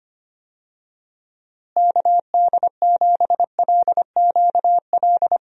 Morse code, heard as a keyed tone, about 25 wpm, 700 Hz. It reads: KD7LQL
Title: USA Call Signs - 6 Characters - Repeated 2x - 25wpm